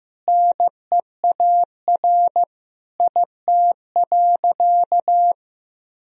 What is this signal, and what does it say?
Morse code, heard as a keyed tone, about 16 wpm, 700 Hz. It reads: NEAR IT.
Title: Sentences from Top 1000 Words - Review Part 1 - 15wpm